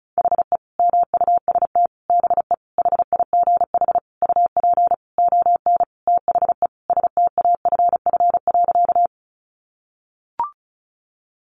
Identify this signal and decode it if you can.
Morse code, heard as a keyed tone, about 34 wpm, 700 Hz. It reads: HE MUST BE HIGH UP ON THE STAFF.
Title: Sentences from Top 1200 Words 35wpm